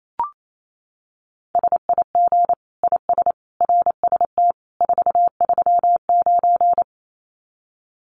Morse code, heard as a keyed tone, about 28 wpm, 700 Hz. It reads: SIG IS RST 439
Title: Common QSO Phrases 28wpm